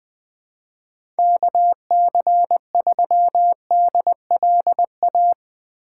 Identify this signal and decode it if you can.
Morse code, heard as a keyed tone, about 20 wpm, 700 Hz. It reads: KC3DLA
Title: USA Call Signs - 3x Character Spacing - 20wpm